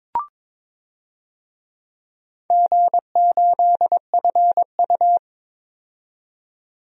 Morse code, heard as a keyed tone, about 22 wpm, 700 Hz. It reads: G8FU
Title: DXCC-100 Call Signs - 4 Characters - 22wpm